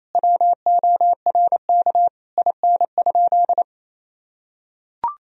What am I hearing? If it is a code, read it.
Morse code, heard as a keyed tone, about 27 wpm, 700 Hz. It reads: WORK IN?
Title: Sentences from Top 1000 Words - Review Part 4 - 28wpm